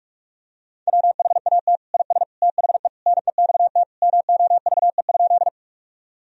Morse code, heard as a keyed tone, about 46 wpm, 700 Hz. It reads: WHAT IS THE NEXT MOVE?
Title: Sentences from Top 1000 Words - Review Part 1 - 45wpm